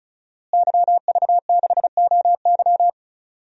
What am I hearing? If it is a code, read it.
Morse code, heard as a keyed tone, about 34 wpm, 700 Hz. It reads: YV6OY
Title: DXCC-100 Call Signs - 5 Characters - Repeated 2x - 35wpm